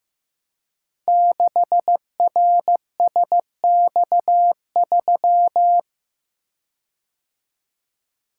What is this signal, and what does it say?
Morse code, heard as a keyed tone, about 15 wpm, 700 Hz. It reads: 6RSX3
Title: Code Groups - 5 Letter:Numbers - 15wpm